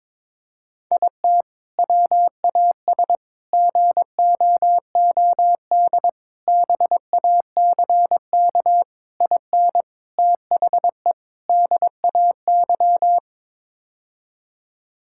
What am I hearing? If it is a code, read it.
Morse code, heard as a keyed tone, about 22 wpm, 700 Hz. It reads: IT WAS GOOD BACK IN THE DAY
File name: practice / 22wpm / Sentences with 3-Word Phrases - 22wpm